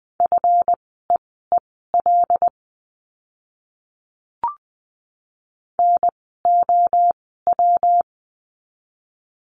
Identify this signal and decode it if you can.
Morse code, heard as a keyed tone, about 20 wpm, spaced as for 15 wpm, 700 Hz. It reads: FEEL NOW
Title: 146 Lesson 24f - Letter y - Words 20wpm